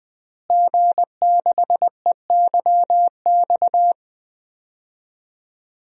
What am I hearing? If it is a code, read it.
Morse code, heard as a keyed tone, about 20 wpm, 700 Hz. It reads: G6EYX
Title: DXCC-100 Call Signs - 5 Characters - Repeated 3x - 20wpm